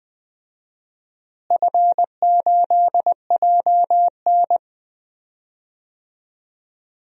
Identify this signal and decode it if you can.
Morse code, heard as a keyed tone, about 20 wpm, 700 Hz. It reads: F8JN
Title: DXCC-100 Call Signs - 4 Characters - Repeated 3x - 20wpm